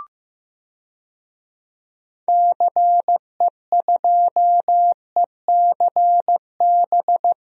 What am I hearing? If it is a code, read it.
Morse code, heard as a keyed tone, about 15 wpm, 700 Hz. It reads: CE2ECB
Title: DXCC-100 Call Signs - 6 Characters - Repeated 3x - 15wpm